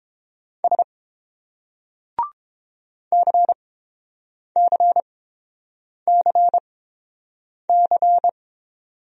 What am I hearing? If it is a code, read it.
Morse code, heard as a keyed tone, about 30 wpm, spaced as for 7 wpm, 700 Hz. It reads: S CCCC